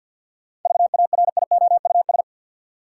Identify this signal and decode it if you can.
Morse code, heard as a keyed tone, about 53 wpm, 700 Hz. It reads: VARIOUS
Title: Sentences from Top 1000 Words - Review Part 4 - New Word Drill - 50wpm